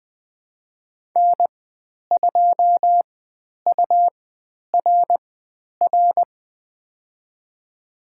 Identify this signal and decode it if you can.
Morse code, heard as a keyed tone, about 20 wpm, spaced as for 10 wpm, 700 Hz. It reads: N2URR
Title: 109 Lesson 18g - Letter m - Call Signs 20wpm